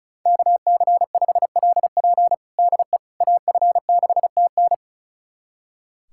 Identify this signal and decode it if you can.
Morse code, heard as a keyed tone, about 36 wpm, 700 Hz. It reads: KC5LP DE AF6TN
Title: USA Callsign DE Callsign - 5 Characters - 35wpm